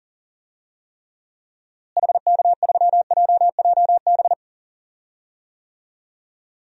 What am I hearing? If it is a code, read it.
Morse code, heard as a keyed tone, about 40 wpm, 700 Hz. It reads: HK3JJB
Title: DXCC-100 Call Signs - 6 Characters - Repeated 2x - 40wpm